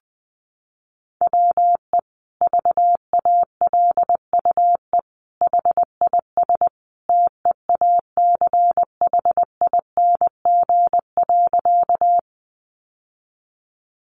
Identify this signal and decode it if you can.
Morse code, heard as a keyed tone, about 20 wpm, 700 Hz. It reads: WE VALUE HIS TEACHING.
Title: Sentences from Top 1100 Words 20wpm